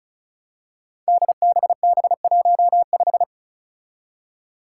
Morse code, heard as a keyed tone, about 34 wpm, 700 Hz. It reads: DBB15
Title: Five Challenging Letter-Number Combinations 35wpm